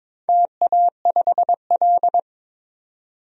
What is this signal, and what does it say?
Morse code, heard as a keyed tone, about 22 wpm, 700 Hz. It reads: TA5L
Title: DXCC-100 Call Signs - 4 Characters - 22wpm